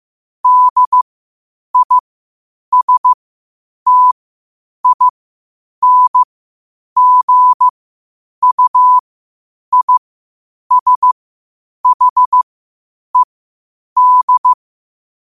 Morse code, heard as a keyed tone, about 15 wpm, spaced as for 8 wpm, 1000 Hz. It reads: DISTINGUISHED
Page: New quotes every day in morse code at 8 Words per minute.